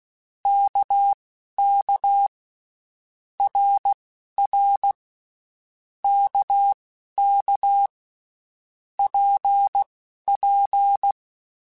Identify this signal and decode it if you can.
Morse code, heard as a keyed tone, about 16 wpm, spaced as for 12 wpm, 800 Hz. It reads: KK RR KK PP
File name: Basic Letter Groups By7Groups / RKPX / RKPX@10